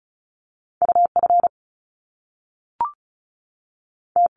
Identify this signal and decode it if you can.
Morse code, heard as a keyed tone, about 36 wpm, spaced as for 6 wpm, 700 Hz. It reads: UFT
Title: Top Two Letter Combinations in Words 35wpm